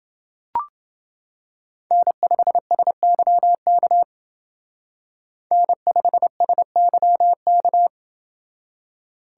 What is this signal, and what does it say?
Morse code, heard as a keyed tone, about 28 wpm, 700 Hz. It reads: N5SYK N5SYK